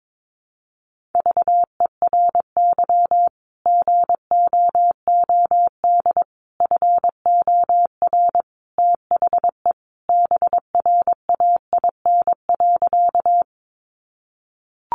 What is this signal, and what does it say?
Morse code, heard as a keyed tone, about 22 wpm, 700 Hz. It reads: VERY GOOD FOR THE BRAIN.
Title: Sentences from Top 1100 Words 22wpm